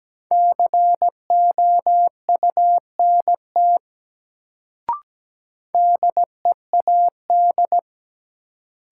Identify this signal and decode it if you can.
Morse code, heard as a keyed tone, about 17 wpm, 700 Hz. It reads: COUNT DEAD
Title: Sentences from Top 1500 Words - New Word Drill - 17wpm